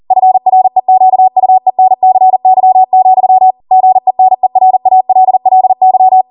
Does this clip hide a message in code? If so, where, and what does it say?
Yes: Morse code from 0.1 to 6.2 s, heard as a keyed tone, about 40 wpm, 750 Hz. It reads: FREQUENCY, GENERALLY